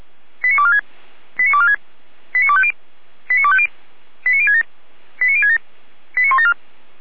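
Five-tone selective call (ZVEI), heard as quick runs of five tones, tones of about 70 ms. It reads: 89246 89246 89250 89250 89067 89067 89174
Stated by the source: ZVEI five-tone call received on SDR, from fire brigade Feuerwehr Orlamünde (f85360).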